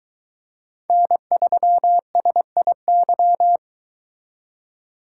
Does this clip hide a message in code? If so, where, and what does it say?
Yes: Morse code from 0.9 to 3.6 s, heard as a keyed tone, about 23 wpm, 700 Hz. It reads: N3SIY